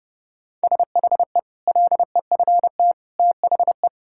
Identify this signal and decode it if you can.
Morse code, heard as a keyed tone, about 30 wpm, 700 Hz. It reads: SHE LEFT THE
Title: Sentences from Top 1600 Words 30wpm